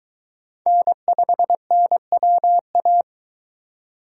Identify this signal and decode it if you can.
Morse code, heard as a keyed tone, about 22 wpm, 700 Hz. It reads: N5NWA